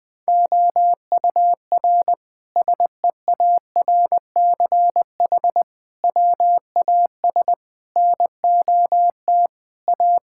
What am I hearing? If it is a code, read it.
Morse code, heard as a keyed tone, about 20 wpm, 700 Hz. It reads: OUR SEARCH WAS NOT A
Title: Sentences from Top 1000 Words - Review Part 5 - 20wpm